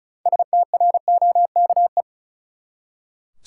Morse code, heard as a keyed tone, about 36 wpm, 700 Hz. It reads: STROKE
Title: Six Letter Words - 35wpm